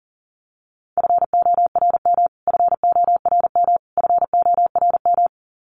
Morse code, heard as a keyed tone, about 40 wpm, 700 Hz. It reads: FORM FORM FORM
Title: Top 300 Words Repeated 40wpm